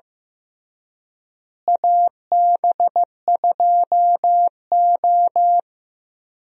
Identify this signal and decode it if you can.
Morse code, heard as a keyed tone, about 15 wpm, 700 Hz. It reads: AB2O
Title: USA Call Signs - 4 Characters - Repeated 3x - 15wpm